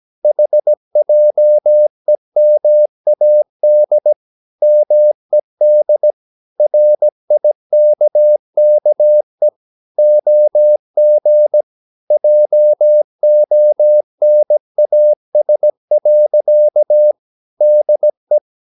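Morse code, heard as a keyed tone, about 17 wpm, 600 Hz. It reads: HJEMAD MED RIKKE OG JONAS. DE
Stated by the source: Slow (QRS) Morse practice at 17 wpm, from CW med Gnister.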